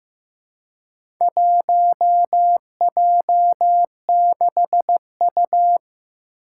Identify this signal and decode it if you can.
Morse code, heard as a keyed tone, about 15 wpm, 700 Hz. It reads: 1J6U
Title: Four Challenging Letter-Number Combinations 15wpm